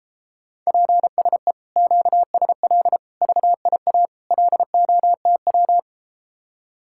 Morse code, heard as a keyed tone, about 33 wpm, 700 Hz. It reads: PSE QSL VIA LOTW